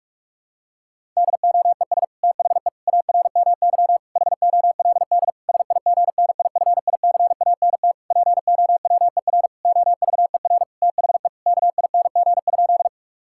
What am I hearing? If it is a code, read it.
Morse code, heard as a keyed tone, about 41 wpm, 700 Hz. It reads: DOES THE ARMY HOLD SIGNIFICANT POWER OVER THE KING?
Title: Sentences from Top 1000 Words - Review Part 6 - 45wpm